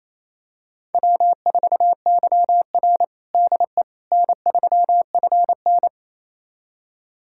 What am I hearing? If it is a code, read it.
Morse code, heard as a keyed tone, about 28 wpm, 700 Hz. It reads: W4YR DE N3FN
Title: USA Callsign DE Callsign - 4 Characters - 28wpm